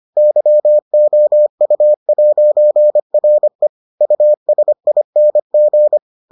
Morse code, heard as a keyed tone, about 25 wpm, 600 Hz. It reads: YOU'RE USING